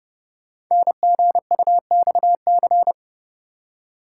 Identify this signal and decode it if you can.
Morse code, heard as a keyed tone, about 29 wpm, 700 Hz. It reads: NGUXC
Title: Code Groups - 5 Letter:Numbers - 30wpm